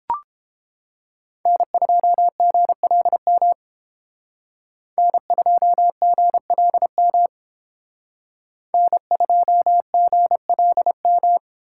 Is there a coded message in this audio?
Yes: Morse code from 1.5 to 11.4 s, heard as a keyed tone, about 31 wpm, 700 Hz. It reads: N2GLM N2GLM N2GLM